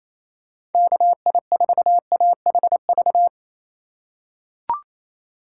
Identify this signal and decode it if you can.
Morse code, heard as a keyed tone, about 28 wpm, 700 Hz. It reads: KI4AHV
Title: USA Call Signs - 6 Characters - 2x Character Spacing - 28wpm